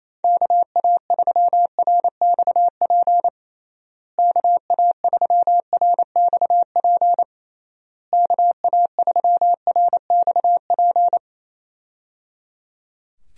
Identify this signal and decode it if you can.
Morse code, heard as a keyed tone, about 28 wpm, 700 Hz. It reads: KA3RXP KA3RXP KA3RXP
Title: USA Call Signs - 6 Characters - Repeated 3x - 28wpm